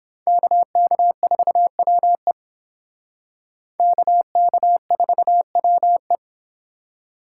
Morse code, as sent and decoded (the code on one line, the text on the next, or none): -.- -.- ....- .-- . / -.- -.- ....- .-- .
KK4WE KK4WE